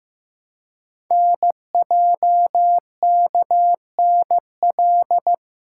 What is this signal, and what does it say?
Morse code, heard as a keyed tone, about 15 wpm, 700 Hz. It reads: NJKNL
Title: Code Groups - 5 Letter:Numbers - 15wpm